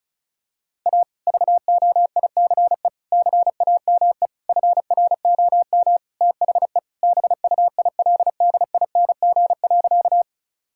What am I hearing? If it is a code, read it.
Morse code, heard as a keyed tone, about 36 wpm, 700 Hz. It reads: A VOICE CAME FROM THE BUILDING.
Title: Sentences from Top 1000 Words - Review Part 5 - 35wpm